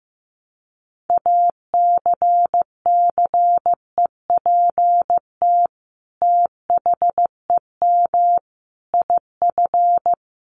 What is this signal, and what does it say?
Morse code, heard as a keyed tone, about 15 wpm, 700 Hz. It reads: ACCEPT THEM IF
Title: Sentences from Top 1300 Words 15wpm